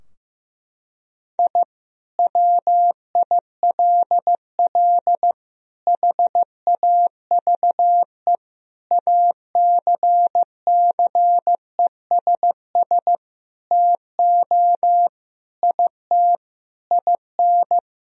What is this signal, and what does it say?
Morse code, heard as a keyed tone, about 16 wpm, 700 Hz. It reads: I WILL HAVE ACCESS TO IT IN
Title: Sentences from Top 1000 Words - Review Part 4 - 15wpm